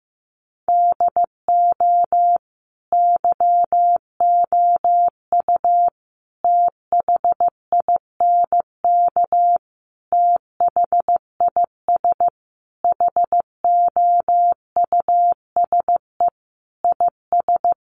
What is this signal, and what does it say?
Morse code, heard as a keyed tone, about 15 wpm, 700 Hz. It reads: DO YOU THINK THIS HOUSE IS
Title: Sentences from Top 1000 Words 15wpm